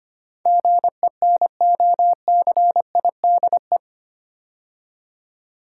Morse code, heard as a keyed tone, about 25 wpm, 700 Hz. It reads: GENOCIDE
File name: 8 Letter Words - First 3 Letters Given - 25wpm